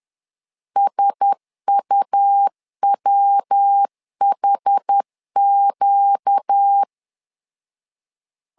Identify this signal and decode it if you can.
Morse code, heard as a keyed tone, about 10 wpm, 800 Hz. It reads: SUWHQ
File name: LEZIONI / 4 Q1S 10wpm 2133